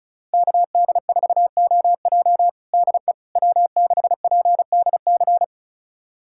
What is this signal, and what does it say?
Morse code, heard as a keyed tone, about 36 wpm, 700 Hz. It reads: KD4OJ DE W6PDC
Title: USA Callsign DE Callsign - 5 Characters - 35wpm